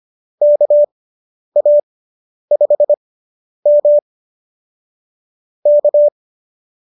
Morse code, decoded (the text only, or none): KA5M K